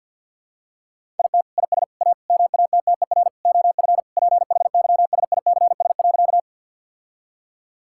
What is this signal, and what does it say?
Morse code, heard as a keyed tone, about 50 wpm, 700 Hz. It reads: IT IS A MATTER OF PHYSICS.
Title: Sentences from Top 1000 Words - Review Part 5 - 50wpm